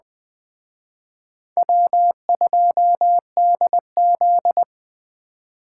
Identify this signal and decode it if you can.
Morse code, heard as a keyed tone, about 21 wpm, 700 Hz. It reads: W2DZ
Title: Speed Racing - USA 1x2 Call Signs 17wpm